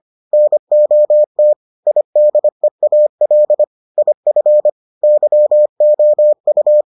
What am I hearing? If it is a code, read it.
Morse code, heard as a keyed tone, about 25 wpm, 600 Hz. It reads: NOT IDEAL IF YOU